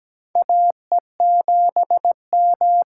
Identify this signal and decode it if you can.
Morse code, heard as a keyed tone, about 17 wpm, 700 Hz. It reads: AE7M
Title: USA Call Signs - 5 Characters - 8x Character Spacing - 17wpm